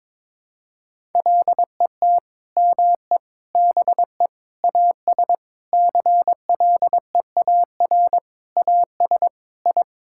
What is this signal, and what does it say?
Morse code, heard as a keyed tone, about 22 wpm, 700 Hz. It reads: LET ME BE AS CLEAR AS I
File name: Sentences with 9-Word Phrases - 22wpm